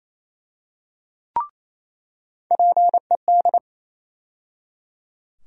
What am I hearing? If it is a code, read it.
Morse code, heard as a keyed tone, about 28 wpm, 700 Hz. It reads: PED
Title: USA Call Sign Postfixes - 3 Letters 28wpm